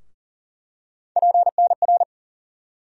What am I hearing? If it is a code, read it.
Morse code, heard as a keyed tone, about 40 wpm, 700 Hz. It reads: PNR
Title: USA Call Sign Postfixes - 3 Letters 40wpm